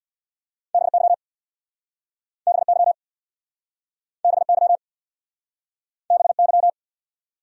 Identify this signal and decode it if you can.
Morse code, heard as a keyed tone, about 67 wpm, 690 Hz. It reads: BY BY BY BY